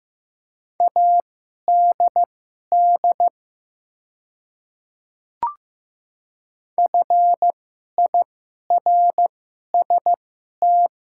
Morse code, heard as a keyed tone, about 15 wpm, spaced as for 11 wpm, 690 Hz. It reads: ADD FIRST
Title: 2x Character Spacing - Top 200 Words - 15wpm